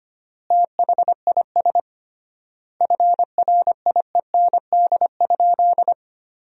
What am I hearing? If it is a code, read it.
Morse code, heard as a keyed tone, about 25 wpm, 700 Hz. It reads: THIS FRIEND?
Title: 3x Word-Spacing - Sentences from Top 300 Words - 25wpm